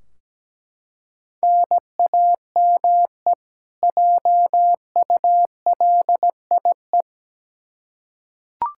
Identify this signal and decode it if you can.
Morse code, heard as a keyed tone, about 17 wpm, 700 Hz. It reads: NAME JULIE
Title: K1USN SST QSO Practice 17wpm